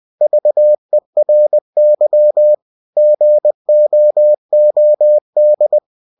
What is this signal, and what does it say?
Morse code, heard as a keyed tone, about 20 wpm, 600 Hz. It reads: VERY GOOD